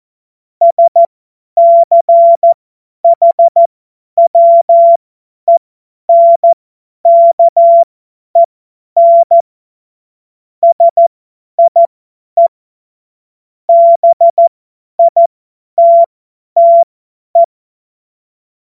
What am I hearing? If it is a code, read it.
Morse code, heard as a keyed tone, about 14 wpm, spaced as for 10 wpm, 680 Hz. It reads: SCHWENKEN SIE BITTE
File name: Morsekurs / Witze / Witz 25. Juli 2023 50 F70